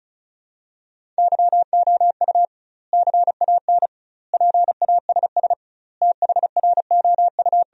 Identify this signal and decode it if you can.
Morse code, heard as a keyed tone, about 36 wpm, 700 Hz. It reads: YOU CAN PASS THROU
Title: 2x Word-Spacing - Sentences from Top 400 Words - 35wpm